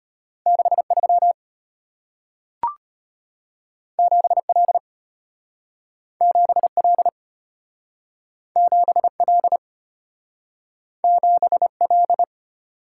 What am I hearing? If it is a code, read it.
Morse code, heard as a keyed tone, about 34 wpm, 700 Hz. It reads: 63 7L 7L 7L 7L